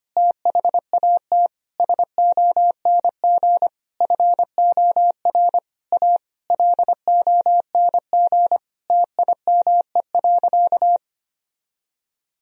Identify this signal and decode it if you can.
Morse code, heard as a keyed tone, about 25 wpm, 700 Hz. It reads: THAT SONG FOR A LONG TIME.
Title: Sentences from Top 1000 Words - Review Part 2 - 25wpm